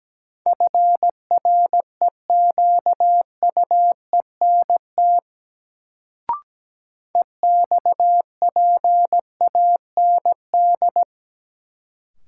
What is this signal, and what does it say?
Morse code, heard as a keyed tone, about 17 wpm, 700 Hz. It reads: FREQUENT EXPAND
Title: Sentences from Top 1400 Words - New Word Drill Alt - 17wpm